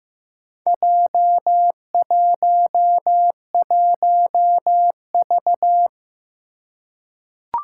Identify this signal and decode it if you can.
Morse code, heard as a keyed tone, about 15 wpm, 700 Hz. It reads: J11V
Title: Four Challenging Letter-Number Combinations 15wpm